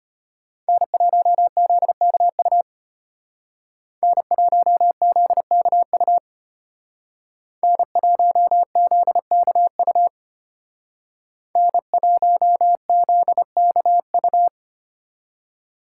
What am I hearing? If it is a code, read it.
Morse code, heard as a keyed tone, about 32 wpm, 700 Hz. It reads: N1ZKU N1ZKU N1ZKU N1ZKU